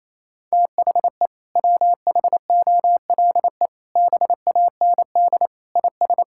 Morse code, heard as a keyed tone, about 28 wpm, 700 Hz. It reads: THE WHOLE BAND IS
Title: Sentences from Top 1600 Words 28wpm